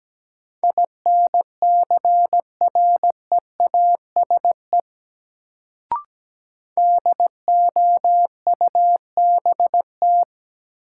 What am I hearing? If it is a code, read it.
Morse code, heard as a keyed tone, about 17 wpm, 700 Hz. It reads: INCREASE DOUBT
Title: Sets of 5 Words - New Word Drill - 17wpm